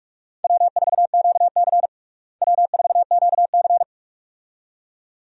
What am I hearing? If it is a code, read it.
Morse code, heard as a keyed tone, about 46 wpm, 700 Hz. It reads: W4QC W4QC
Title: USA Call Signs - 4 Characters - Repeated 2x - 45wpm